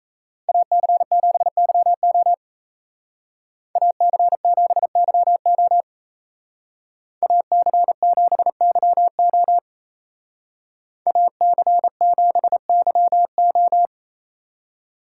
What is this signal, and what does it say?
Morse code, heard as a keyed tone, about 36 wpm, 700 Hz. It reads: AC7YO AC7YO AC7YO AC7YO